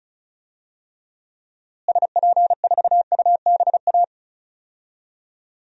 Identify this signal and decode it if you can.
Morse code, heard as a keyed tone, about 36 wpm, 700 Hz. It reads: SP4UBA
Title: DXCC-100 Call Signs - 6 Characters - Repeated 2x - 35wpm